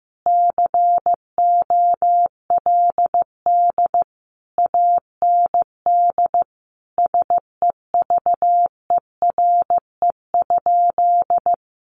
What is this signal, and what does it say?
Morse code, heard as a keyed tone, about 15 wpm, 700 Hz. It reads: COLD AND SEVERE?
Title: Sentences from Top 1400 Words 15wpm